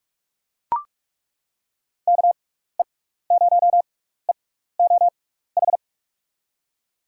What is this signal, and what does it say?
Morse code, heard as a keyed tone, about 42 wpm, spaced as for 15 wpm, 700 Hz. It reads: KE0EOH
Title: USA Call Signs - 6 Characters - 6x Character Spacing - 45wpm